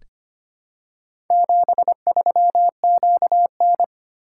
Morse code, heard as a keyed tone, about 25 wpm, 700 Hz. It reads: 73QN
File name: practice / 25wpm / Four Letter-Number Combinations - 25wpm